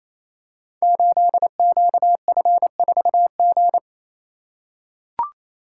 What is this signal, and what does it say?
Morse code, heard as a keyed tone, about 28 wpm, 700 Hz. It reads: 8QF4G
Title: Code Groups - 5 Letter:Numbers - 28wpm